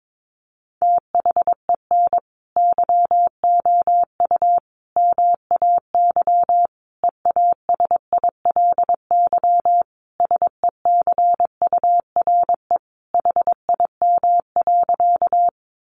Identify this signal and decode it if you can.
Morse code, heard as a keyed tone, about 22 wpm, 700 Hz. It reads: THEN YOU MAY EASILY SECURE HIM.
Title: Sentences from Top 1200 Words 22wpm